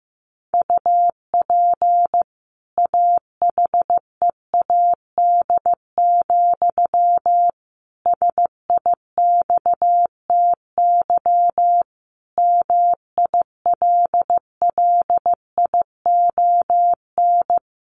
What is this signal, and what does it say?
Morse code, heard as a keyed tone, about 15 wpm, 700 Hz. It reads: UP AHEAD, SIXTY MILLION
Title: A Sound Of Thunder 15wpm